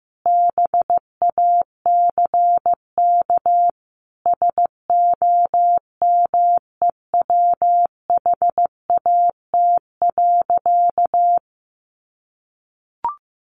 Morse code, heard as a keyed tone, about 15 wpm, 700 Hz. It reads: BACK SOMEWHAT.
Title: Sentences from Top 1300 Words 15wpm